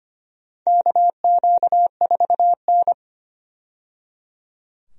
Morse code, heard as a keyed tone, about 25 wpm, 700 Hz. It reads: KQ4N
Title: USA Call Signs - 4 Characters - 25wpm